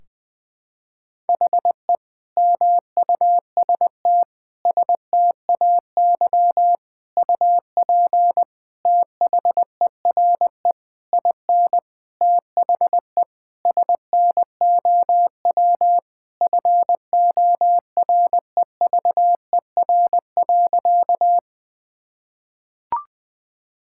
Morse code, heard as a keyed tone, about 21 wpm, 700 Hz. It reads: HE MUST STAY UP THERE IN THE SNOW FOREVER.
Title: Sentences from Top 1600 Words 20wpm